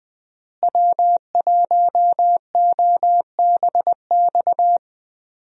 Morse code, heard as a keyed tone, about 20 wpm, 700 Hz. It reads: W1OBX
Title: USA Call Signs - 5 Characters - Repeated 3x - 20wpm